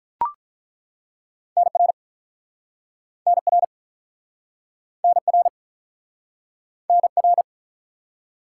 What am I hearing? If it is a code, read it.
Morse code, heard as a keyed tone, about 41 wpm, 700 Hz. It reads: NR NR NR NR
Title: Speed-Racing Top 180 QSO Elements 35wpm